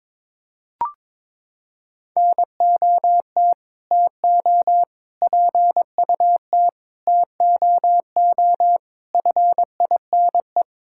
Morse code, heard as a keyed tone, about 22 wpm, 700 Hz. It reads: NOT TO PUT TOO FINE
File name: practice / 22wpm / Sets of 8 Words - 22wpm